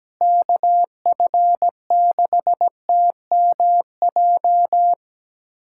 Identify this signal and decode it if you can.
Morse code, heard as a keyed tone, about 17 wpm, 700 Hz. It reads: KF6TMJ
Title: USA Callsign DE Callsign - 6 Characters - 17wpm